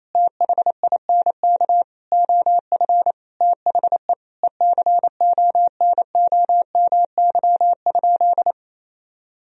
Morse code, heard as a keyed tone, about 27 wpm, 690 Hz. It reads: THINK OF THE ECONOMY?
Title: Sentences from Top 1000 Words - Review Part 3 - 28wpm